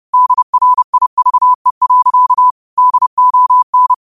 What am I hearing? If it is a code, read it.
Morse code, heard as a keyed tone, about 30 wpm, 1000 Hz. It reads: DRIVE. DON